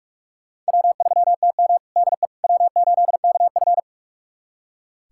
Morse code, heard as a keyed tone, about 46 wpm, 700 Hz. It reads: W3TM DE W8KF
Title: USA Callsign DE Callsign - 4 Characters - 45wpm